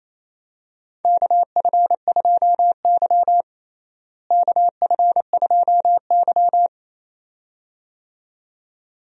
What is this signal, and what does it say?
Morse code, heard as a keyed tone, about 27 wpm, 700 Hz. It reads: KF2Y KF2Y
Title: USA Call Signs - 4 Characters - Repeated 2x - 28wpm